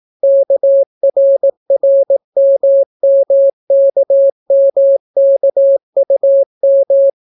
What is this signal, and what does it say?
Morse code, heard as a keyed tone, about 18 wpm, 550 Hz. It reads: KRRMMKMKUM